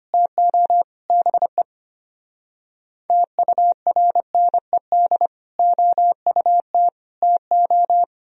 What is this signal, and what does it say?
Morse code, heard as a keyed tone, about 32 wpm, 700 Hz. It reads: TO BE TURNED OUT TO